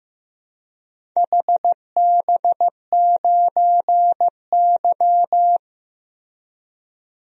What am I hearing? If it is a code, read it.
Morse code, heard as a keyed tone, about 15 wpm, 700 Hz. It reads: HB9Y
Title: DXCC-100 Call Signs - 4 Characters - 15wpm